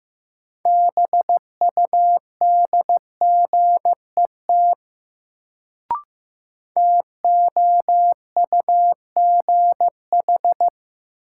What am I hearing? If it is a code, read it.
Morse code, heard as a keyed tone, about 15 wpm, 700 Hz. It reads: BUDGET TOUGH
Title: Sentences from Top 1200 Words - New Word Drill - 15wpm